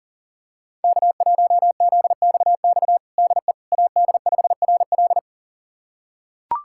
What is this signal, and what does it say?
Morse code, heard as a keyed tone, about 40 wpm, 700 Hz. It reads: K1ZXX DE AD5RL
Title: USA Callsign DE Callsign - 5 Characters - 40wpm